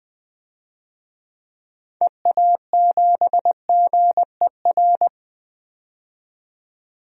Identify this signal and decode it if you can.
Morse code, heard as a keyed tone, about 20 wpm, 700 Hz. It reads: EA7GER
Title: DXCC-100 Call Signs - 6 Characters - 20wpm